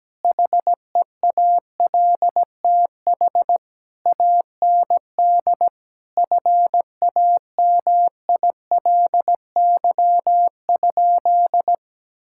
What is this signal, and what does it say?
Morse code, heard as a keyed tone, about 17 wpm, 700 Hz. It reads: HEALTH AND FAMILY?
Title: Sentences from Top 1600 Words 17wpm